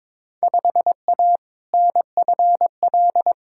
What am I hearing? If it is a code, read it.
Morse code, heard as a keyed tone, about 22 wpm, 700 Hz. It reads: 5A NFL
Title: Field Day Exchanges - 22wpm